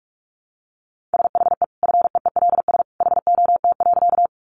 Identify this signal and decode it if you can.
Morse code, heard as a keyed tone, about 44 wpm, 700 Hz. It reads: SHE FEELS HOT.
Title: Easy Phrases 45wpm